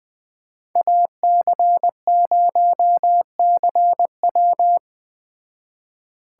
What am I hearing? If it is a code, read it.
Morse code, heard as a keyed tone, about 20 wpm, 700 Hz. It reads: AC0CW